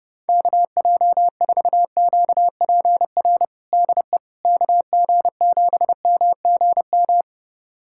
Morse code, heard as a keyed tone, about 30 wpm, 700 Hz. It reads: KJ4QPR DE KG7MGM
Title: USA Callsign DE Callsign 30wpm